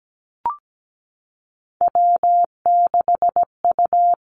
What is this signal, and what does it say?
Morse code, heard as a keyed tone, about 17 wpm, 700 Hz. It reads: W6U
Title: Three Challenging Letter-Number Combinations 17wpm